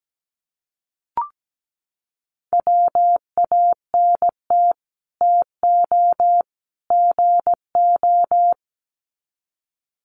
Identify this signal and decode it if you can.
Morse code, heard as a keyed tone, about 17 wpm, 700 Hz. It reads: WANT TO GO
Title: Sets of 3 Words 17wpm